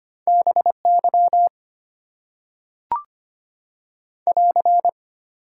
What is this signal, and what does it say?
Morse code, heard as a keyed tone, about 25 wpm, 690 Hz. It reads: BY <AR>
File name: Top 90 QSO Elements - 6x Character Spacing - 25wpm